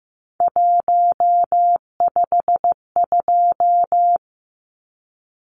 Morse code, heard as a keyed tone, about 15 wpm, 700 Hz. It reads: 152
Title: Numbers - 3-Digit 15wpm